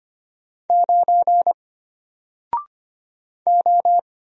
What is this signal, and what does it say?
Morse code, heard as a keyed tone, about 25 wpm, 700 Hz. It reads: 9 O
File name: Single-Letter-Number-25wpm